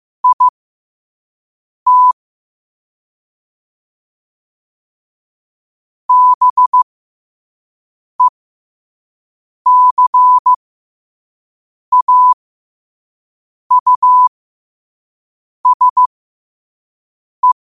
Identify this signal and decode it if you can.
Morse code, heard as a keyed tone, about 15 wpm, spaced as for 5 wpm, 1000 Hz. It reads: IT BECAUSE
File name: qotdfiles / 5wpm